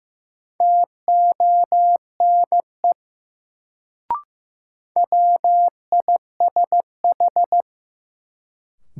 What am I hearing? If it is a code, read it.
Morse code, heard as a keyed tone, about 15 wpm, 700 Hz. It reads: TONE WISH
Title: Sentences from Top 1600 Words - New Word Drill Alt - 15wpm